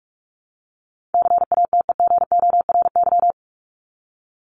Morse code, heard as a keyed tone, about 44 wpm, 700 Hz. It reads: CATEGORY
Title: Top 751-1000 Words 45wpm